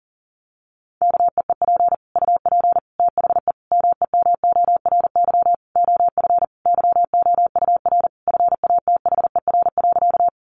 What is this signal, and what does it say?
Morse code, heard as a keyed tone, about 40 wpm, 700 Hz. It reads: KEEP UP THE MEMORY OF YOUR FATHER.
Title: Sentences from Top 1300 Words 40wpm